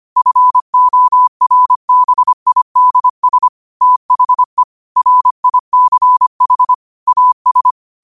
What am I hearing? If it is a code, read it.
Morse code, heard as a keyed tone, about 25 wpm, 1000 Hz. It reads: FORBIDS THE RICH AS
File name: 25wpm